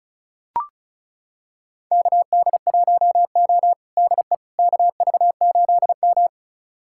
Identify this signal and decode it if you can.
Morse code, heard as a keyed tone, about 36 wpm, 690 Hz. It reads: KD1O DE KV8M
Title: USA Callsign DE Callsign - 4 Characters - 35wpm